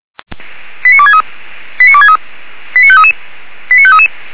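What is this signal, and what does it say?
Five-tone selective call (ZVEI), heard as quick runs of five tones, tones of about 70 ms. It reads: 89263 89263 89533 89533
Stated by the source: ZVEI five-tone selective call received with an SDR.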